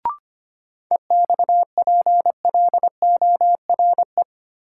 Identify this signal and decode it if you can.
Morse code, heard as a keyed tone, about 25 wpm, 700 Hz. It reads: EXPLORE
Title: Sentences from Top 1100 Words - New Word Drill Alt - 25wpm